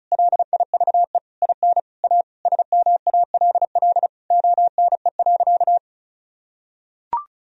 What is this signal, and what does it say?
Morse code, heard as a keyed tone, about 34 wpm, 700 Hz. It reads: LIVE IN A SMALL ONE.
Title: Sentences from Top 1000 Words - Review Part 1 - 35wpm